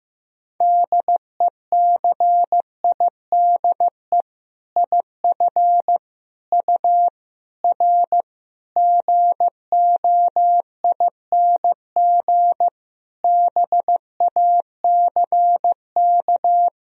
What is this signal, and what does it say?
Morse code, heard as a keyed tone, about 15 wpm, 700 Hz. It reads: DECIDE IF U R GOING BACK
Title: Sentences from Top 500 Words and Abbreviations 15wpm